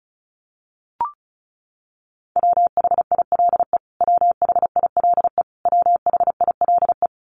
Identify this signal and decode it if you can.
Morse code, heard as a keyed tone, about 34 wpm, 700 Hz. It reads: WHILE WHILE WHILE
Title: Top 500 Words Repeated 35wpm